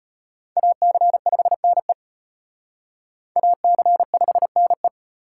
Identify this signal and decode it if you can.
Morse code, heard as a keyed tone, about 31 wpm, 700 Hz. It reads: AC5NE AC5NE
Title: Speed Racing - USA 2x2 Call Signs 25wpm